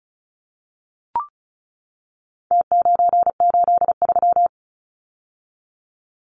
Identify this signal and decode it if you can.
Morse code, heard as a keyed tone, about 34 wpm, 700 Hz. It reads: T983
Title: Numbers - Contest Style - 4-Digit 35wpm